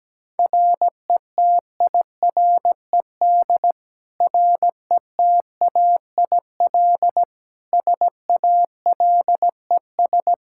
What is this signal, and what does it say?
Morse code, heard as a keyed tone, about 17 wpm, 700 Hz. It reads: RETIRED RETAIL SALES
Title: Practice QSO - Exchange 3 - More Info - 17wpm